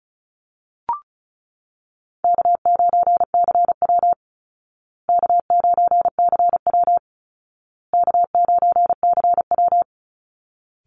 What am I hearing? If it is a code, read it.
Morse code, heard as a keyed tone, about 34 wpm, 700 Hz. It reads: K9CW K9CW K9CW
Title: Easy Call Signs Repeated 3x - 35wpm